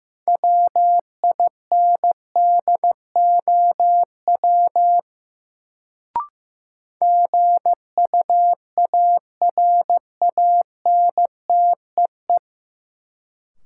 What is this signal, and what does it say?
Morse code, heard as a keyed tone, about 15 wpm, 700 Hz. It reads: WINDOW GUARANTEE
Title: Sentences from Top 1600 Words - New Word Drill Alt - 15wpm